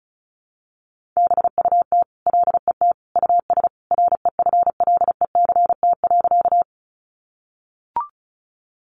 Morse code, heard as a keyed tone, about 34 wpm, 700 Hz. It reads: BUT LET US REFLECT.
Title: Sentences from Top 1500 Words 35wpm